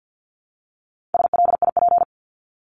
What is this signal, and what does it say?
Morse code, heard as a keyed tone, about 50 wpm, 700 Hz. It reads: SLIP
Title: Top 1751-2000 Words 50wpm